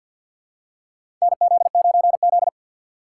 Morse code, heard as a keyed tone, about 52 wpm, 700 Hz. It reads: NZ9Z
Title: USA Call Signs - 2x Character Spacing - 50wpm